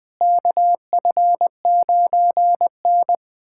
Morse code, heard as a keyed tone, about 21 wpm, 700 Hz. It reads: KF9N
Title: USA Callsign DE Callsign - 4 Characters - 20wpm